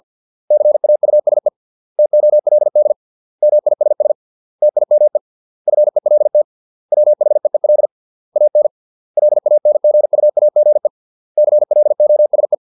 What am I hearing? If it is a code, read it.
Morse code, heard as a keyed tone, about 50 wpm, 600 Hz. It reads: CAUSE TOLD MISS TIME FELT WHEEL AN LANGUAGE CLOSE